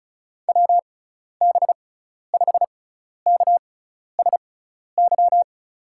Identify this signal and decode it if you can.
Morse code, heard as a keyed tone, about 36 wpm, spaced as for 12 wpm, 700 Hz. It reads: WB5KSY
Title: USA Call Signs - 6 Characters - 6x Character Spacing - 35wpm